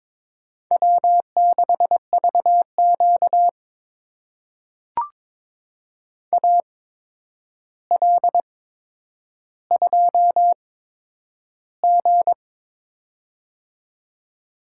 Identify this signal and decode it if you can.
Morse code, heard as a keyed tone, about 22 wpm, spaced as for 9 wpm, 700 Hz. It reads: W6VQ AL2G
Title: USA Call Signs - 4 Characters - 8x Character Spacing - 22wpm